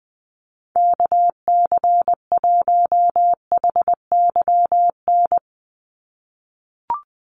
Morse code, heard as a keyed tone, about 20 wpm, 700 Hz. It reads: KC1HYN
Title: USA Call Signs 20wpm